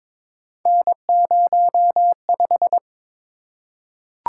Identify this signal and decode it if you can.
Morse code, heard as a keyed tone, about 22 wpm, 700 Hz. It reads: N05
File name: practice / 22wpm / Three Letter-Number Combinations - 22wpm